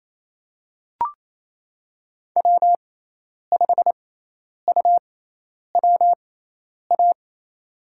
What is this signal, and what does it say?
Morse code, heard as a keyed tone, about 28 wpm, spaced as for 10 wpm, 700 Hz. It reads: W5UWA
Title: USA Call Signs - 6x Character Spacing - 28wpm